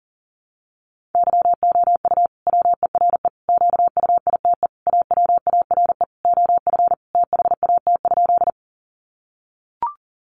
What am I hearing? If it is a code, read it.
Morse code, heard as a keyed tone, about 40 wpm, 700 Hz. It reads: YOU WERE QUITE AWARE OF THAT?
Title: Sentences from Top 1200 Words 40wpm